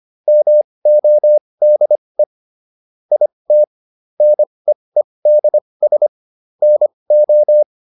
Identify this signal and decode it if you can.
Morse code, heard as a keyed tone, about 25 wpm, spaced as for 20 wpm, 600 Hz. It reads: MODE IT NEEDS NO